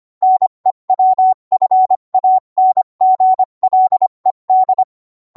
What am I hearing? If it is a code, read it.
Morse code, heard as a keyed tone, about 25 wpm, 750 Hz. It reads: NEWFANGLED